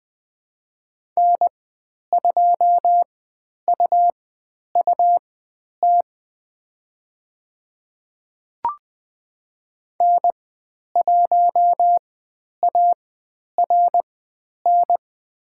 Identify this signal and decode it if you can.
Morse code, heard as a keyed tone, about 20 wpm, spaced as for 10 wpm, 700 Hz. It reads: N2UUT N1ARN
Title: 102 Lesson 17g - Letter u - Call Signs 20wpm